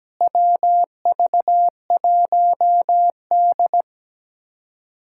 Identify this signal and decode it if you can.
Morse code, heard as a keyed tone, about 17 wpm, 700 Hz. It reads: WV1D
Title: K1USN SST QSO Practice - 17wpm